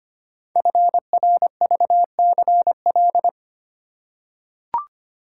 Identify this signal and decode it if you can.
Morse code, heard as a keyed tone, about 25 wpm, 700 Hz. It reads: FRVCL
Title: Code Groups - 5 Letters - 25wpm